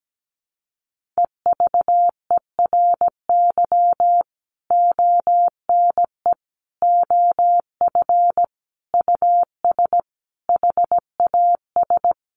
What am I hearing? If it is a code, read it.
Morse code, heard as a keyed tone, about 17 wpm, 700 Hz. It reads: EVERY ONE OF US HAS
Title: Sentences from Top 1500 Words 17wpm